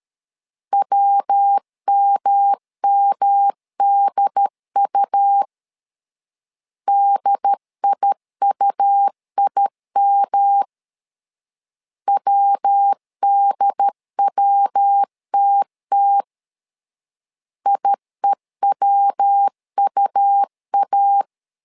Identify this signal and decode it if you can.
Morse code, heard as a keyed tone, about 13 wpm, 800 Hz. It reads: WMMDU DIUIM WDWTT IEWUA
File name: LEZIONI / 3 MWRO / 3 MW 12wpm 1633